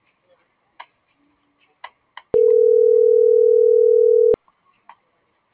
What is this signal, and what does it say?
Ringback tone (North America), 440 + 480 Hz.